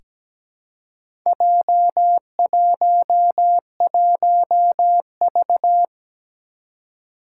Morse code, heard as a keyed tone, about 17 wpm, 700 Hz. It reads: J11V
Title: Four Challenging Letter-Number Combinations 17wpm